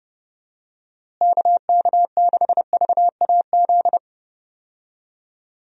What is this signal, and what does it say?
Morse code, heard as a keyed tone, about 30 wpm, 700 Hz. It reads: KK6VAZ
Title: USA Call Signs - 6 Characters - Repeated 3x - 30wpm